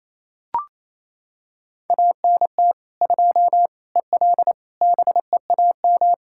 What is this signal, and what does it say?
Morse code, heard as a keyed tone, about 28 wpm, 700 Hz. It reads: ANT 2 EL BEAM
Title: Practice QSO - Exchange 2 - Info - 28wpm